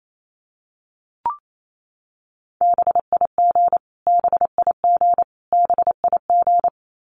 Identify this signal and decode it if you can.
Morse code, heard as a keyed tone, about 28 wpm, 700 Hz. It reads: BIG BIG BIG
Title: Top 500 Words Repeated 28wpm